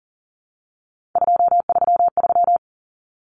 Morse code, heard as a keyed tone, about 40 wpm, 700 Hz. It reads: 233
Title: Numbers - 3-Digit 40wpm